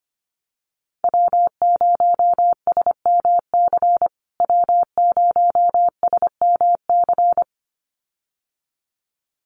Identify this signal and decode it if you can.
Morse code, heard as a keyed tone, about 25 wpm, 700 Hz. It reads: W0SMC W0SMC
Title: USA Call Signs Repeated 2x 25wpm